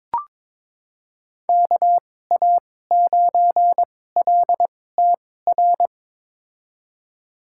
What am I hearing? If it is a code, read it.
Morse code, heard as a keyed tone, about 22 wpm, spaced as for 16 wpm, 700 Hz. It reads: KA9LTR
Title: USA Call Signs - 2x Character Spacing - 22wpm